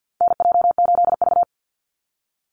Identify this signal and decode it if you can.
Morse code, heard as a keyed tone, about 48 wpm, 700 Hz. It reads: NJ8V
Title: Call Signs Repeated 2x - 50wpm